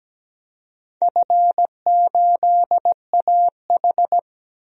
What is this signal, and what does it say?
Morse code, heard as a keyed tone, about 17 wpm, 700 Hz. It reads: F8AH
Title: DXCC-100 Call Signs - 4 Characters - 17wpm